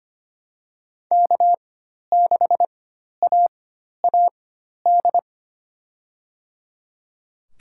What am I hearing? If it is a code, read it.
Morse code, heard as a keyed tone, about 25 wpm, spaced as for 12 wpm, 700 Hz. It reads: K6AAD
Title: USA Call Signs - 5 Characters - 4x Character Spacing - 25wpm